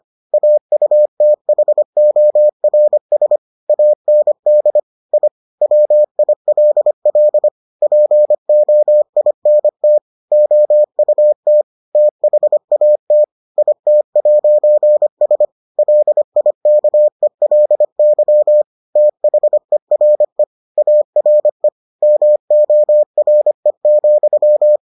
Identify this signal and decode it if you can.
Morse code, heard as a keyed tone, about 25 wpm, 600 Hz. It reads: AUTHORS AND I WILL POINT OUT THAT IT'S LIKELY THERE ARE MORE,